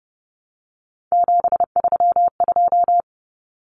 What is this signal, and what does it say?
Morse code, heard as a keyed tone, about 30 wpm, 700 Hz. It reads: 732
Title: Common Cut Numbers - 3-Digit 30wpm